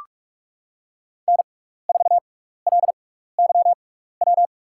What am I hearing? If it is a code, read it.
Morse code, heard as a keyed tone, about 44 wpm, spaced as for 15 wpm, 700 Hz. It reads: N4LYW
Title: USA Call Signs - 6x Character Spacing - 45wpm